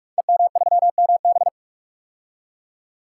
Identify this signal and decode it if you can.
Morse code, heard as a keyed tone, about 44 wpm, 700 Hz. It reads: EM3MB
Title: Code Groups - 5 Letter:Numbers - 45wpm